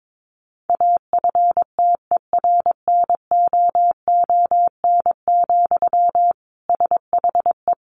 Morse code, heard as a keyed tone, about 22 wpm, 700 Hz. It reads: AFTERNOON, SHE
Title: Sentences from Top 1500 Words 22wpm